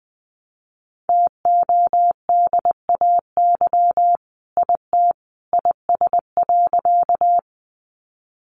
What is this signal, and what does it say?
Morse code, heard as a keyed tone, about 20 wpm, 700 Hz. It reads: TODAY IT IS.
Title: Sentences from Top 600 Words 20wpm